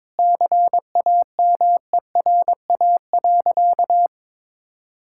Morse code, heard as a keyed tone, about 22 wpm, 700 Hz. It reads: CAMERA.
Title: Sentences from Top 1000 Words - Review Part 5 - 22wpm